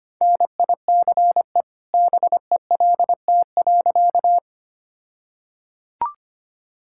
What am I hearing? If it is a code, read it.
Morse code, heard as a keyed tone, about 25 wpm, 700 Hz. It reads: NICE BELT.
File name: practice / 25wpm / All-Age Appropriate Jokes - 25wpm